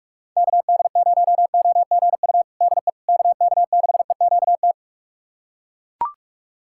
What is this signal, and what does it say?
Morse code, heard as a keyed tone, about 42 wpm, 700 Hz. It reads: KD0OGU DE KK6EQT
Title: USA Callsign DE Callsign - 6 Characters - 45wpm